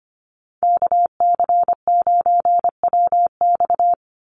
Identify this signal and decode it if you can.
Morse code, heard as a keyed tone, about 25 wpm, 700 Hz. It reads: KC9WX
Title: USA Call Signs 25wpm